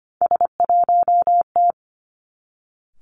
Morse code, heard as a keyed tone, about 25 wpm, 700 Hz. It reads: S1T
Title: Three Challenging Letter-Number Combinations 25wpm